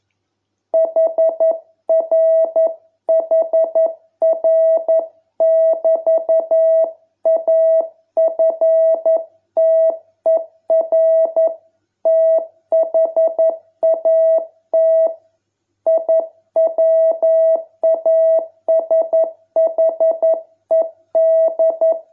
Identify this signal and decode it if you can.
Morse code, heard as a keyed tone, about 11 wpm, 660 Hz. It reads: HRHR<BT>AFTER THAT I WASHED